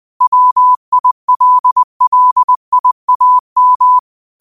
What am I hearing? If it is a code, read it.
Morse code, heard as a keyed tone, about 20 wpm, 1000 Hz. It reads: WILLIAM
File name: qotdfiles / 20wpm